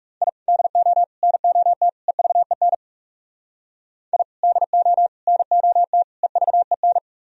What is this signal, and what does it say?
Morse code, heard as a keyed tone, about 35 wpm, 700 Hz. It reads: I DO NOT EVEN I DO NOT EVEN
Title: Speed-Racing - Sets of 4 Words 30wpm